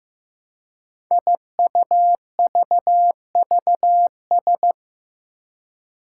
Morse code, heard as a keyed tone, about 15 wpm, 700 Hz. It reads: IUVVS
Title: Five Challenging Letter-Number Combinations 15wpm